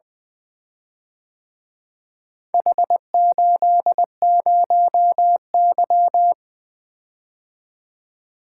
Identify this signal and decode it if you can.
Morse code, heard as a keyed tone, about 23 wpm, 700 Hz. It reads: H80Y
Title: Speed-Racing - Four Letter-Number Combinations - 20wpm